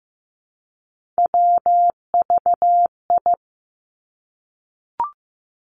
Morse code, heard as a keyed tone, about 15 wpm, 700 Hz. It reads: WVI
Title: Three Challenging Letter-Number Combinations 15wpm